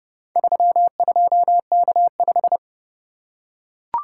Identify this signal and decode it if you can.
Morse code, heard as a keyed tone, about 28 wpm, 700 Hz. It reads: 32K5